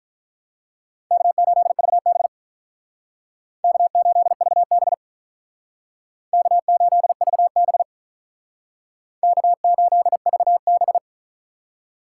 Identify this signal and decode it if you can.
Morse code, heard as a keyed tone, about 43 wpm, 700 Hz. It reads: K8VB K8VB K8VB K8VB
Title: Speed Racing - USA 1x2 Call Signs 35wpm